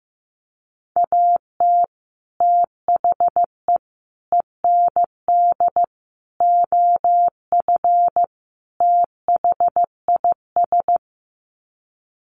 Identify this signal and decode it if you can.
Morse code, heard as a keyed tone, about 15 wpm, 700 Hz. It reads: AT THE END OF THIS
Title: Sets of 5 Words 15wpm